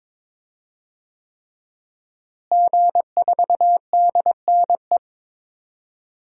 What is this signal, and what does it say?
Morse code, heard as a keyed tone, about 22 wpm, 700 Hz. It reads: G4DNE
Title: DXCC-100 Call Signs - 5 Characters - Repeated 3x - 22wpm